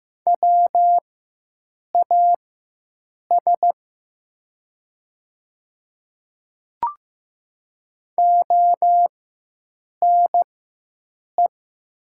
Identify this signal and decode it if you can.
Morse code, heard as a keyed tone, about 15 wpm, spaced as for 7 wpm, 700 Hz. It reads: WAS ONE
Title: 4x Character Spacing - Top 100 Words - 15wpm